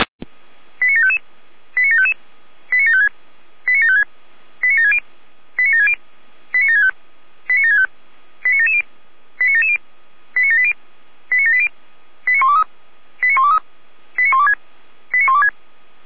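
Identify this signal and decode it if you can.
Five-tone selective call (ZVEI), heard as quick runs of five tones, tones of about 70 ms. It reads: 89755 89755 89756 89756 89760 89760 89765 89765 89770 89770 89780 89780 89123 89123 89126 89126
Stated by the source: ZVEI five-tone call received on SDR.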